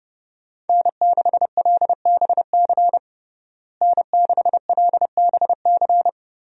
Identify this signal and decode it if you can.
Morse code, heard as a keyed tone, about 30 wpm, 700 Hz. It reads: N6LBC N6LBC
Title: USA Call Signs - 5 Characters - Repeated 3x - 30wpm